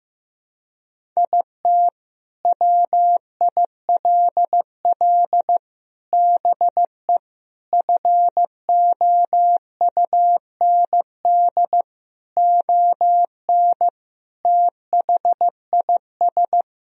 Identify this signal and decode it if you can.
Morse code, heard as a keyed tone, about 15 wpm, 700 Hz. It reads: IT WILL BE FOUND ON THIS
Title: Sentences from Top 1000 Words - Review Part 1 - 15wpm